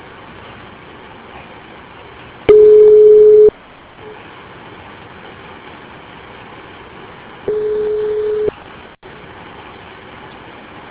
Ringback tone (Europe, 425 Hz).